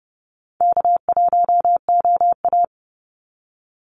Morse code, heard as a keyed tone, about 30 wpm, 700 Hz. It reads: K1OA
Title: Easy Call Signs - 30wpm